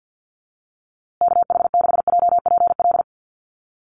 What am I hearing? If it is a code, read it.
Morse code, heard as a keyed tone, about 50 wpm, 700 Hz. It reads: KH6JPL
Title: USA Call Signs Repeated 3x 50wpm